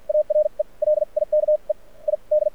Morse code, heard as a keyed tone, about 53 wpm, spaced as for 42 wpm, 600 Hz. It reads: ARE LIKE I D
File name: archived / CRSnet